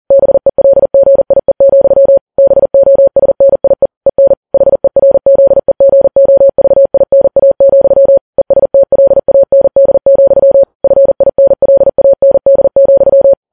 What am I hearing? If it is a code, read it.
Morse code, heard as a keyed tone, about 40 wpm, 550 Hz. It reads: BELGIE, BOSNIE EN HERZEGOVINA, ESTLAND, FINLAND,
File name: Files Web tls / Morse on9fv / morse40kps